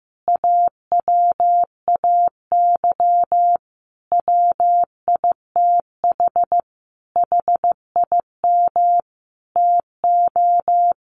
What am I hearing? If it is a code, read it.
Morse code, heard as a keyed tone, about 15 wpm, 700 Hz. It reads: AWAY WITH HIM TO
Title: Sentences from Top 1100 Words 15wpm